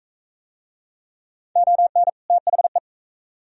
Morse code, heard as a keyed tone, about 43 wpm, 700 Hz. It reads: ON THE